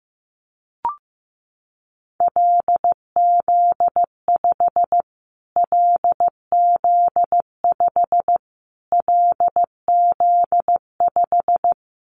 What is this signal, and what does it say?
Morse code, heard as a keyed tone, about 15 wpm, 700 Hz. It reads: LZ5 LZ5 LZ5
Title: Callsign Prefixes for Top 25 DXCC - Repeated 3x - 15wpm